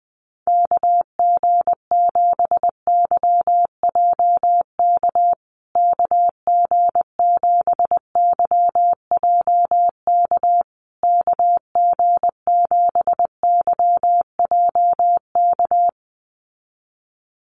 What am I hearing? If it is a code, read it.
Morse code, heard as a keyed tone, about 19 wpm, 700 Hz. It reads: KG7YJK KG7YJK KG7YJK
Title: USA Call Signs Repeated 3x 20wpm